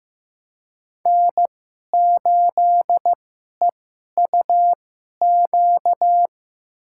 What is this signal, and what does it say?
Morse code, heard as a keyed tone, about 15 wpm, spaced as for 11 wpm, 700 Hz. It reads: N8EUQ
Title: USA Call Signs - 5 Characters - 2x Character Spacing - 15wpm